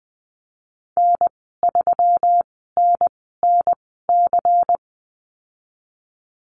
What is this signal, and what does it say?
Morse code, heard as a keyed tone, about 19 wpm, spaced as for 14 wpm, 700 Hz. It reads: N3NNC
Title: 122 Lesson 20g - Number 3 - Call Signs 20wpm